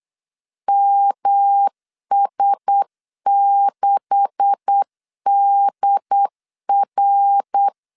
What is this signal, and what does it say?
Morse code, heard as a keyed tone, about 8 wpm, 800 Hz. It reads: MS6DR
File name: LEZIONI / 10 SIGNS / 10C DOT 8wpm 3649